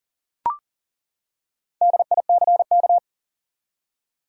Morse code, heard as a keyed tone, about 40 wpm, 700 Hz. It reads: DICK
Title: USA Common Names - 40wpm